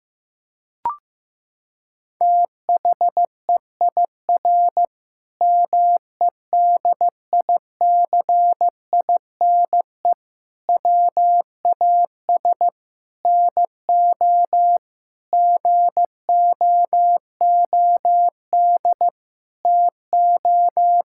Morse code, heard as a keyed tone, about 15 wpm, 700 Hz. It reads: THEIR MEDICINE WAS NO GOOD TO
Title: Sentences from Top 1000 Words - Review Part 6 - 15wpm